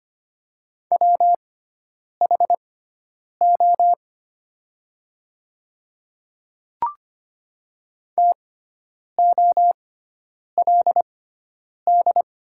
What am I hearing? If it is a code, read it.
Morse code, heard as a keyed tone, about 25 wpm, spaced as for 9 wpm, 700 Hz. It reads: WHO TOLD
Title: 6x Character Spacing - Top 500 Words - 25wpm